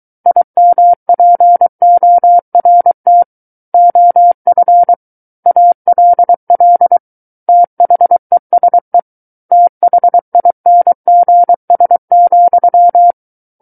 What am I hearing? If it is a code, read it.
Morse code, heard as a keyed tone, about 23 wpm, 700 Hz. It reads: IMPORT OF ALL THESE THINGS,
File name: Book One / 14-Chapter 14 - 23 WPM